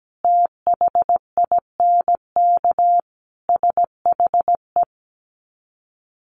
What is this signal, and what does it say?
Morse code, heard as a keyed tone, about 17 wpm, 700 Hz. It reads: THINK SHE
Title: Sets of 5 Words 17wpm